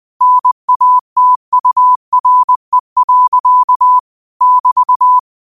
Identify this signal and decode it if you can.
Morse code, heard as a keyed tone, about 20 wpm, 1000 Hz. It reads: NATURE. <BT>